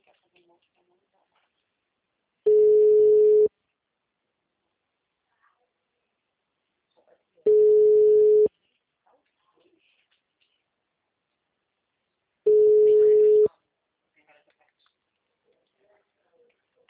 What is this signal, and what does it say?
Ringback tone (Europe, 425 Hz), with speech.
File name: out- -2051- - - .65082